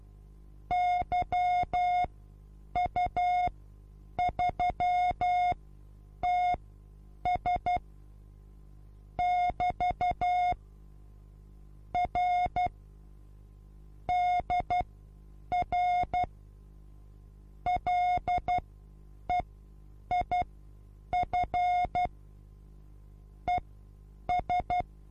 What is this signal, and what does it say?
Morse code, heard as a keyed tone, about 12 wpm, spaced as for 8 wpm, 720 Hz. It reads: YU3TS <BT> R DR LEIF ES
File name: Onlinelehrgang / morsen / DJ4UF Morsen23